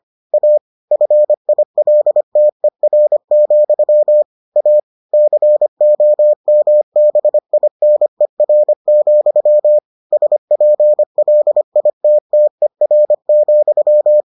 Morse code, heard as a keyed tone, about 25 wpm, 600 Hz. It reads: A FILTER, A COMBINER, SPLITTER,